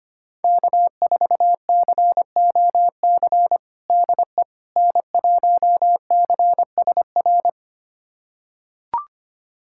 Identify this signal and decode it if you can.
Morse code, heard as a keyed tone, about 25 wpm, 700 Hz. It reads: K4COC DE N1CSR
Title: USA Callsign DE Callsign 25wpm